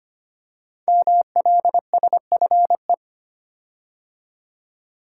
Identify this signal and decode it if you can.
Morse code, heard as a keyed tone, about 25 wpm, 700 Hz. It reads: MLSFE
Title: Code Groups - 5 Letters - 25wpm